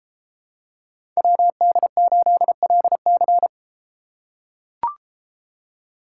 Morse code, heard as a keyed tone, about 32 wpm, 700 Hz. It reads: WD8LC